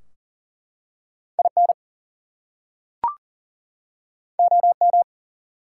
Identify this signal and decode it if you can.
Morse code, heard as a keyed tone, about 41 wpm, 700 Hz. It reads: IN OM
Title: Top 180 QSO Elements 40wpm